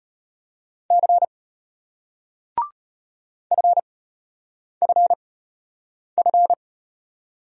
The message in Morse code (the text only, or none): C FFF